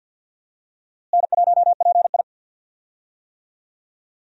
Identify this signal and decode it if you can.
Morse code, heard as a keyed tone, about 52 wpm, 700 Hz. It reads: N1PI
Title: USA Call Signs - 4 Characters - Repeated 2x - 50wpm